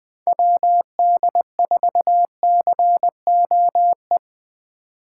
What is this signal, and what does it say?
Morse code, heard as a keyed tone, about 24 wpm, 700 Hz. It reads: WD4COE
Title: Speed Racing - USA Call Signs 20wpm